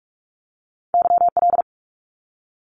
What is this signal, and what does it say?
Morse code, heard as a keyed tone, about 44 wpm, 700 Hz. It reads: YL
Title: Abbreviations Repeated 3x - 45wpm